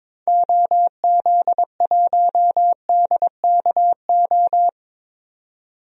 Morse code, heard as a keyed tone, about 22 wpm, 700 Hz. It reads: OZ1DKO
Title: DXCC-100 Call Signs - 6 Characters - 22wpm